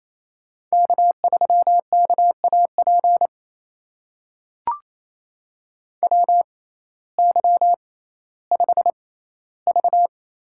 Morse code, heard as a keyed tone, about 28 wpm, spaced as for 15 wpm, 700 Hz. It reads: K3KAP WY5V
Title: USA Call Signs - 6x Character Spacing - 28wpm